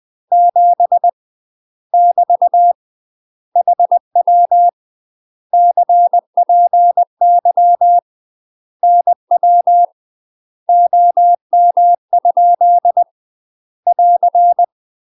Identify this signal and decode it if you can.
Morse code, heard as a keyed tone, about 20 wpm, 700 Hz. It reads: 7 <BT> HW CPY NW OM? <AR>
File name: UK MASTER QSOs TRAINING / 20WPM / UK MASTER QSOs 0003